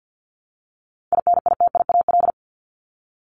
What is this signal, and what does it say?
Morse code, heard as a keyed tone, about 50 wpm, 700 Hz. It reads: INITIAL
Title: Top 1501-1750 Words 50wpm